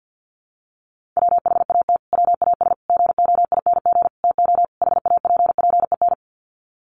Morse code, heard as a keyed tone, about 49 wpm, 700 Hz. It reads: WHAT WAS GOING TO HAPPEN
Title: Sets of 5 Words Encore 50wpm